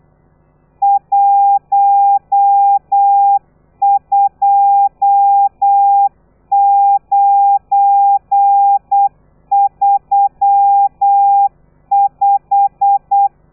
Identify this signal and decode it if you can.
Morse code, heard as a keyed tone, about 8 wpm, 800 Hz. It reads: 12935